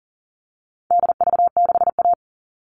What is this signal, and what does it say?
Morse code, heard as a keyed tone, about 40 wpm, 700 Hz. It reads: DV6A
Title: All Cut Numbers - 4-Digit 40wpm